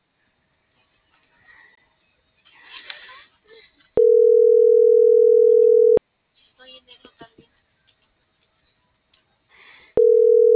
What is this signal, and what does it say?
Ringback tone (North America), 440 + 480 Hz.